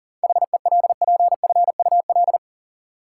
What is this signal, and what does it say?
Morse code, heard as a keyed tone, about 37 wpm, 700 Hz. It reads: HELPFUL
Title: Sentences from Top 1000 Words - Review Part 6 - New Word Drill - 40wpm